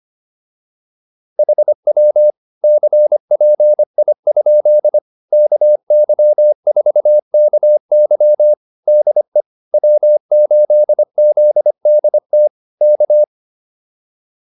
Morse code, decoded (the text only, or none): HW CPI? KY4KY DE W8ZDT K